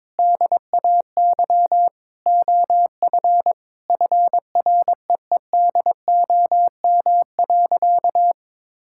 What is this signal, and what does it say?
Morse code, heard as a keyed tone, about 22 wpm, 700 Hz. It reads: DAY OF FREEDOM.
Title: Sentences from Top 1000 Words - Review Part 5 - 22wpm